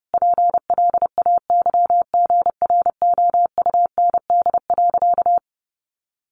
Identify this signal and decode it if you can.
Morse code, heard as a keyed tone, about 30 wpm, 700 Hz. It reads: PLAYGROUND.
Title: Additional Easy Phrases 30wpm